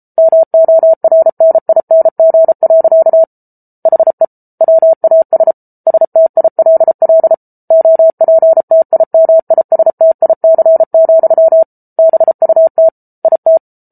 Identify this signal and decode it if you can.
Morse code, heard as a keyed tone, about 34 wpm, 650 Hz. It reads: MORNING. HE WAS STILL OPTIMISTIC, BUT IT